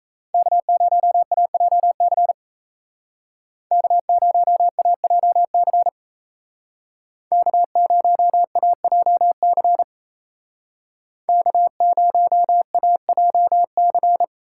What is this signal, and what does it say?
Morse code, heard as a keyed tone, about 36 wpm, 700 Hz. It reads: K0AJC K0AJC K0AJC K0AJC